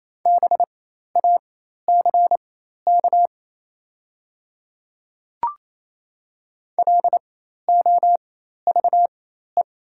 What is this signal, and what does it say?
Morse code, heard as a keyed tone, about 28 wpm, spaced as for 13 wpm, 690 Hz. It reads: BACK LOVE
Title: 4x Character Spacing - Top 500 Words - 28wpm